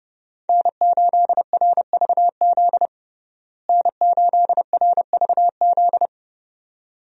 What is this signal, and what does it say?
Morse code, heard as a keyed tone, about 32 wpm, 700 Hz. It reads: N8RVZ N8RVZ
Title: USA Call Signs - 5 Characters - Repeated 2x - 30wpm